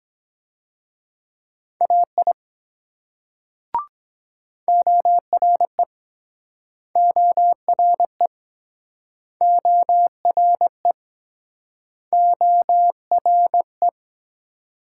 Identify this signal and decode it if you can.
Morse code, heard as a keyed tone, about 24 wpm, 700 Hz. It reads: AI ORE ORE ORE ORE